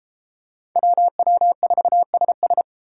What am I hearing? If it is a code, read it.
Morse code, heard as a keyed tone, about 33 wpm, 700 Hz. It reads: WW4SS